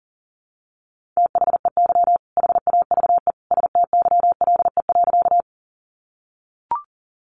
Morse code, heard as a keyed tone, about 40 wpm, 700 Hz. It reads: THEY HAVE STYLE.
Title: Sentences from Top 800 Words 40wpm